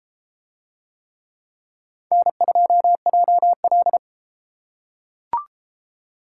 Morse code, heard as a keyed tone, about 31 wpm, 700 Hz. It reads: N2JL